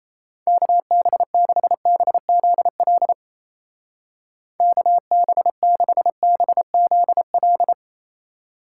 Morse code, heard as a keyed tone, about 36 wpm, 700 Hz. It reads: KB6BZL KB6BZL